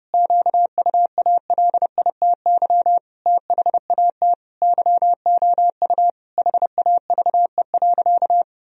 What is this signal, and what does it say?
Morse code, heard as a keyed tone, about 30 wpm, 700 Hz. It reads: QUALITY THAT YOU HAVE.
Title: Sentences from Top 1000 Words - Review Part 4 - 30wpm